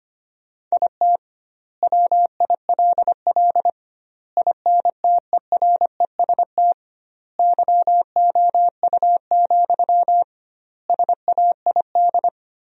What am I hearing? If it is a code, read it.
Morse code, heard as a keyed tone, about 25 wpm, 700 Hz. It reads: IT WILL INTEREST YOU, SAID
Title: 2x Word-Spacing - Sentences from Top 400 Words - 25wpm